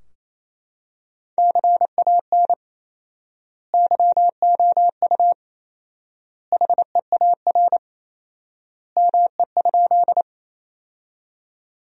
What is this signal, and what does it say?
Morse code, heard as a keyed tone, about 27 wpm, 700 Hz. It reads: CAN YOU HEAR ME?
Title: 4x Word-Spacing - Sentences from Top 300 Words - 28wpm